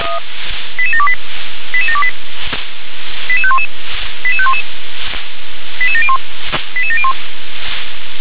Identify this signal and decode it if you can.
Five-tone selective call (ZVEI), heard as quick runs of five tones, tones of about 70 ms. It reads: 88728 88728 88511 88511 88701 88701